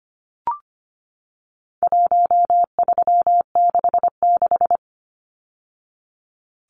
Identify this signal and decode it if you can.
Morse code, heard as a keyed tone, about 25 wpm, 700 Hz. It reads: 1366
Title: Numbers - Contest Style - 4-Digit 25wpm